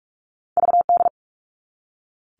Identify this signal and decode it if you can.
Morse code, heard as a keyed tone, about 46 wpm, 700 Hz. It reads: VD
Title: All Cut Numbers - 2-Digit 45wpm